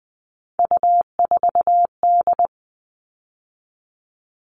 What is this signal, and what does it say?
Morse code, heard as a keyed tone, about 20 wpm, 700 Hz. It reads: U4D
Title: All Cut Numbers - 3-Digit 20wpm